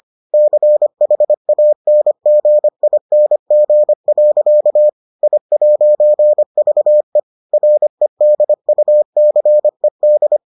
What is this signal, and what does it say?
Morse code, heard as a keyed tone, about 25 wpm, 600 Hz. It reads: CHANGING. I'VE REDUCED